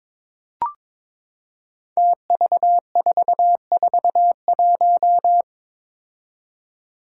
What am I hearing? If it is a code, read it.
Morse code, heard as a keyed tone, about 22 wpm, 700 Hz. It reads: TV441
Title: Five Challenging Letter-Number Combinations 22wpm